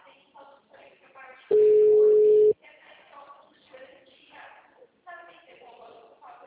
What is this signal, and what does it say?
Ringback tone (Europe, 425 Hz), with speech.
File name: out- -2002- - - .69981